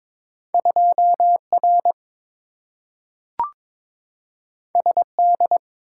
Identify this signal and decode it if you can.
Morse code, heard as a keyed tone, about 22 wpm, 700 Hz. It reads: 2R SD
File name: practice / 22wpm / Two Letter-Number Combinations - 22wpm